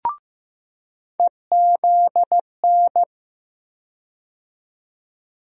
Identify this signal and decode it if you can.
Morse code, heard as a keyed tone, about 15 wpm, 700 Hz. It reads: EZN
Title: USA Call Sign Postfixes - 3 Letters 15wpm